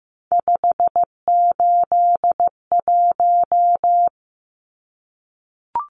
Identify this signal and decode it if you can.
Morse code, heard as a keyed tone, about 15 wpm, 700 Hz. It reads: 581
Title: Numbers - 3-Digit 15wpm